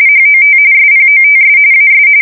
RTTY, about 45.45 baud. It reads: HOT SPOTS AS
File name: fsk170 / usnavy5